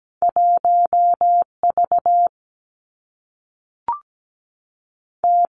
Morse code, heard as a keyed tone, about 17 wpm, 700 Hz. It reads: 1 V T
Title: Two Challenging Letter-Number Combinations 17wpm